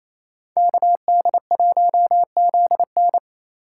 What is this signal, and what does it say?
Morse code, heard as a keyed tone, about 35 wpm, 700 Hz. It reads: KD1ZN